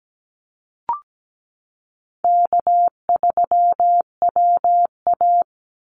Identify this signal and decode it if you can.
Morse code, heard as a keyed tone, about 17 wpm, 700 Hz. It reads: K3WA
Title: Call Signs Repeated 2x - 17wpm